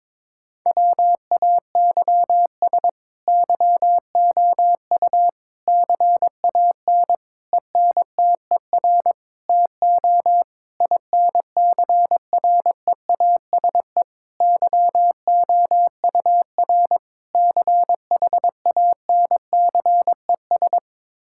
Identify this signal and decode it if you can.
Morse code, heard as a keyed tone, about 21 wpm, 700 Hz. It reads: WAYS YOU CAN ENTER TO INCREASE YOUR CHANCES
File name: practice / 22wpm / Sets of 8 Words - 22wpm